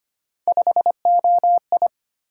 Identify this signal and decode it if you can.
Morse code, heard as a keyed tone, about 25 wpm, 700 Hz. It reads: 5OI
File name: Three Letter-Number Combinations - 25wpm